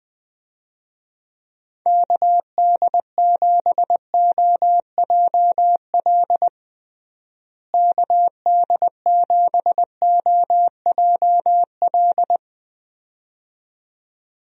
Morse code, heard as a keyed tone, about 20 wpm, 700 Hz. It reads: KD7OJL KD7OJL
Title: USA Call Signs - 6 Characters - Repeated 2x - 20wpm